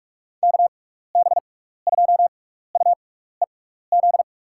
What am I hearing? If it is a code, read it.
Morse code, heard as a keyed tone, about 43 wpm, spaced as for 15 wpm, 700 Hz. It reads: KB2UEZ
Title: USA Call Signs - 6 Characters - 6x Character Spacing - 45wpm